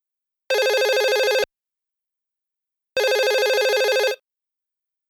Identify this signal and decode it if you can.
There is no code.